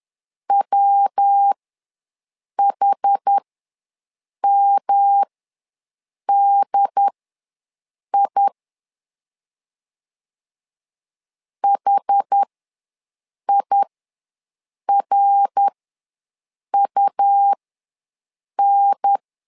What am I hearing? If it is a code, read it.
Morse code, heard as a keyed tone, about 10 wpm, spaced as for 6 wpm, 800 Hz. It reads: WHMDI HIRUN
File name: LEZIONI / 3 MWRO / 3 MWR ws 10wpm 2151